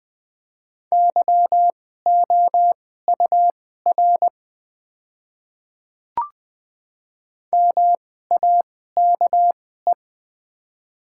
Morse code, heard as a keyed tone, about 20 wpm, spaced as for 14 wpm, 690 Hz. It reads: YOUR MAKE
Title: 2x Character Spacing - Top 100 Words - 20wpm